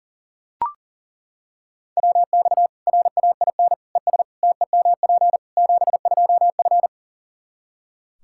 Morse code, heard as a keyed tone, about 40 wpm, 700 Hz. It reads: WX RAIN ES TEMP 72F
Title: K1USN SST QSO Practice 40wpm